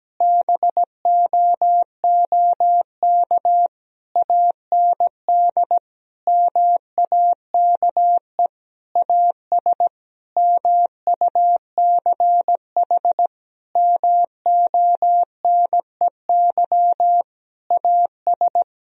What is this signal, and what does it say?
Morse code, heard as a keyed tone, about 17 wpm, 700 Hz. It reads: BOOK AND MAKE AS MUCH MONEY AS
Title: Sentences from Top 1000 Words - Review Part 3 - 17wpm